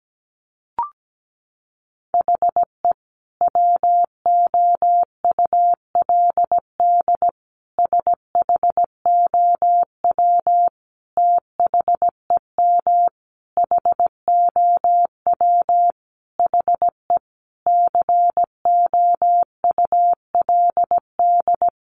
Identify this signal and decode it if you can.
Morse code, heard as a keyed tone, about 17 wpm, 700 Hz. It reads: HE WOULD SHOW THEM HOW HE COULD
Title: Sentences from Top 1300 Words 17wpm